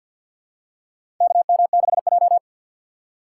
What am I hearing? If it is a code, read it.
Morse code, heard as a keyed tone, about 48 wpm, 700 Hz. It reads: KM6J
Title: USA Call Signs - 4 Characters - 2x Character Spacing - 50wpm